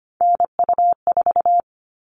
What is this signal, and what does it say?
Morse code, heard as a keyed tone, about 25 wpm, 700 Hz. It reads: NU4
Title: All Cut Numbers - 3-Digit 25wpm